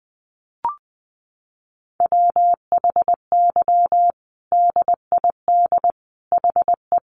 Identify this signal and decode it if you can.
Morse code, heard as a keyed tone, about 20 wpm, 700 Hz. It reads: WHY DID HE
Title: Sentences from Top 1100 Words 20wpm